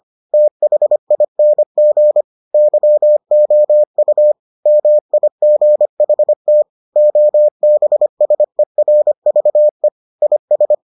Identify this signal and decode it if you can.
Morse code, heard as a keyed tone, about 25 wpm, 600 Hz. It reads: THING YOU MIGHT OBSERVE IS